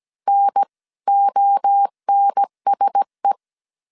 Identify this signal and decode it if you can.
Morse code, heard as a keyed tone, about 17 wpm, 800 Hz. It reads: NONSE
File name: 16wpm